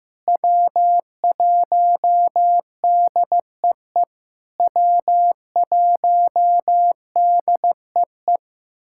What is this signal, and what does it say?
Morse code, heard as a keyed tone, about 15 wpm, 700 Hz. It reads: W1DEE W1DEE
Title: SOTA QSO Practice 15wpm